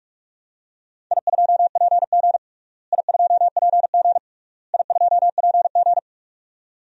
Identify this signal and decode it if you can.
Morse code, heard as a keyed tone, about 42 wpm, 700 Hz. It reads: I2PG I2PG I2PG
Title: DXCC-100 Call Signs - 4 Characters - Repeated 3x - 45wpm